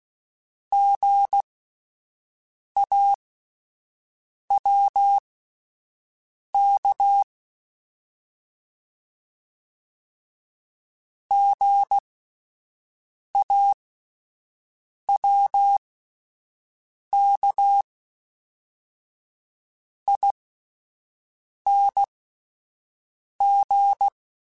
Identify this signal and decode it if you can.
Morse code, heard as a keyed tone, about 16 wpm, spaced as for 5 wpm, 800 Hz. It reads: GAWK GAWKING